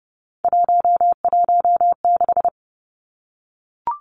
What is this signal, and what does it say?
Morse code, heard as a keyed tone, about 30 wpm, 700 Hz. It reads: 116
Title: Numbers - 3-Digit 30wpm